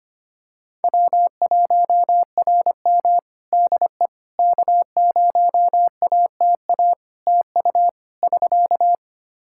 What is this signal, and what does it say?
Morse code, heard as a keyed tone, about 25 wpm, 700 Hz. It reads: W1RM DE K0ATA TU <SK>
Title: Practice QSO - Exchange 4 - Ending - 25wpm